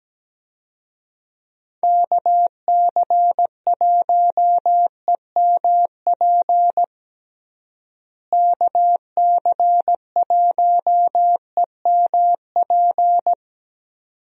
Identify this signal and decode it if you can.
Morse code, heard as a keyed tone, about 17 wpm, 700 Hz. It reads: KC1EMP KC1EMP
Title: USA Call Signs - 6 Characters - Repeated 2x - 17wpm